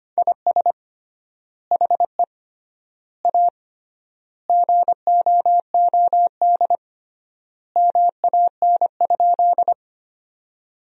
Morse code, heard as a keyed tone, about 25 wpm, 700 Hz. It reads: IS HE A GOOD MAN?
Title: 3x Word-Spacing - Sentences from Top 200 Words - 25wpm